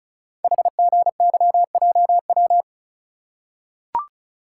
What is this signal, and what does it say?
Morse code, heard as a keyed tone, about 33 wpm, 690 Hz. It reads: HGYJW
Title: Code Groups - 5 Letters - 35wpm